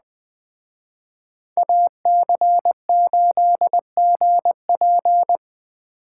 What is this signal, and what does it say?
Morse code, heard as a keyed tone, about 20 wpm, 700 Hz. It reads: AC8GP